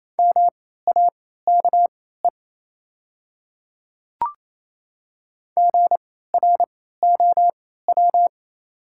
Morse code, heard as a keyed tone, about 28 wpm, spaced as for 16 wpm, 690 Hz. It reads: MAKE GROW
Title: 3x Character Spacing - Top 500 Words - 28wpm